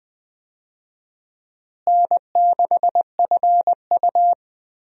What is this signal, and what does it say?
Morse code, heard as a keyed tone, about 21 wpm, 700 Hz. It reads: N6FU
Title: USA Call Signs - 4 Characters - Repeated 3x - 20wpm